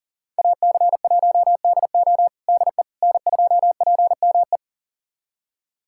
Morse code, heard as a keyed tone, about 40 wpm, 700 Hz. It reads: AC1DO DE N2PME
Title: USA Callsign DE Callsign - 5 Characters - 40wpm